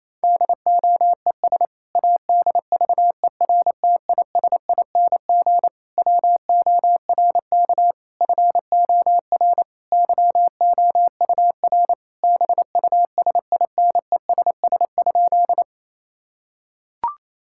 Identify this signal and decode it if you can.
Morse code, heard as a keyed tone, about 28 wpm, 700 Hz. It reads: DOES ADVERTISING WORK FOR YOUR BUSINESS?
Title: Sentences from Top 1000 Words - Review Part 6 - 28wpm